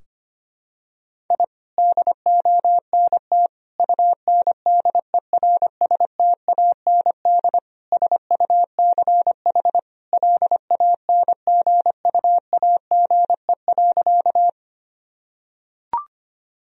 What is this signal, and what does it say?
Morse code, heard as a keyed tone, about 25 wpm, 700 Hz. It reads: I DONT UNDERSTAND SUCH LANGUAGE.
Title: Sentences from Top 1000 Words - Review Part 3 - 25wpm